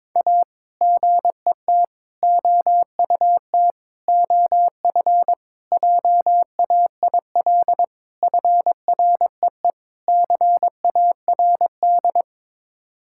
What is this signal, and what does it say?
Morse code, heard as a keyed tone, about 22 wpm, 700 Hz. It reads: A GET OUT OF JAIL FREE CARD
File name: practice / 22wpm / Sets of 7 Words - 22wpm